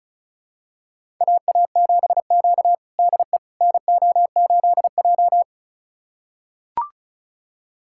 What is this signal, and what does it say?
Morse code, heard as a keyed tone, about 34 wpm, 700 Hz. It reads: AA7Q DE NO8J
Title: USA Callsign DE Callsign - 4 Characters - 35wpm